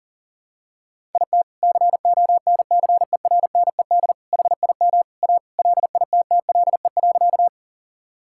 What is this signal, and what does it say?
Morse code, heard as a keyed tone, about 40 wpm, 700 Hz. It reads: IT CONCERNED HIM A LITTLE.
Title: Sentences from Top 1000 Words - Review Part 6 - 40wpm